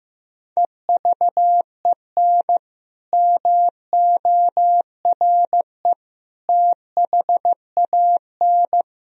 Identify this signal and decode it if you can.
Morse code, heard as a keyed tone, about 15 wpm, 700 Hz. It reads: EVEN MORE THAN
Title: Sentences from Top 1000 Words - Review Part 1 - 15wpm